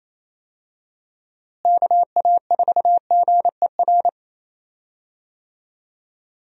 Morse code, heard as a keyed tone, about 28 wpm, 700 Hz. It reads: KA4GER
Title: USA Call Signs - 6 Characters - Repeated 2x - 28wpm